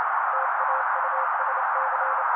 Morse code, heard as a keyed tone, about 29 wpm, 530 Hz. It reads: TAUSC
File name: daily / 27wpm noise